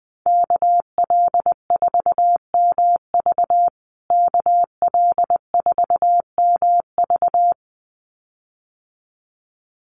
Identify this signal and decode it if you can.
Morse code, heard as a keyed tone, about 20 wpm, 700 Hz. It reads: KL4MV KL4MV
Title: USA Call Signs Repeated 2x 20wpm